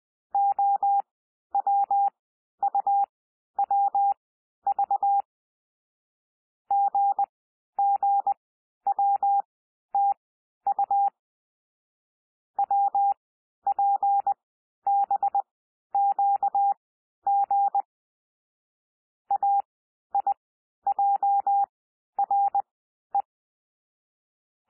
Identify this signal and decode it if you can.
Morse code, heard as a keyed tone, about 20 wpm, spaced as for 11 wpm, 800 Hz. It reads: OWUWV GGWTU WPBQG AIJRE
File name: cwmp3 / L3 SOUND LETTERS 20 wpm